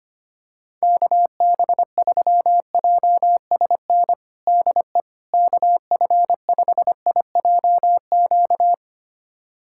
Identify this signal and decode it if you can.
Morse code, heard as a keyed tone, about 25 wpm, 700 Hz. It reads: KB3JSN DE KF5IJQ
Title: USA Callsign DE Callsign - 6 Characters - 25wpm